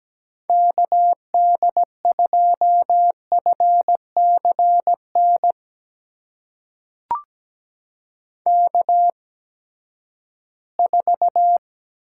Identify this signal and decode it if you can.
Morse code, heard as a keyed tone, about 17 wpm, spaced as for 12 wpm, 700 Hz. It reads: KD2FCN K4
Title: USA Call Signs - 8x Character Spacing - 17wpm